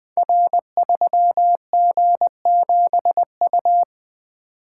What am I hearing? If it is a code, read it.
Morse code, heard as a keyed tone, about 20 wpm, 700 Hz. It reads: R3G7U
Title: Code Groups - 5 Letter:Numbers - 20wpm